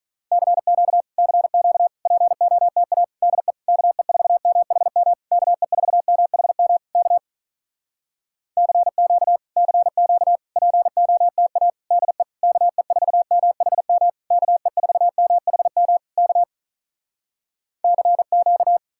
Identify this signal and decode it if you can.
Morse code, heard as a keyed tone, about 46 wpm, 700 Hz. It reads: CQ CQ POTA DE KE4MHM KE4MHM K CQ CQ POTA DE KE4MHM KE4MHM K CQ